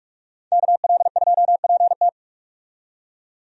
Morse code, heard as a keyed tone, about 46 wpm, 700 Hz. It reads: KL2PT
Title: USA Call Signs - 5 Characters - 45wpm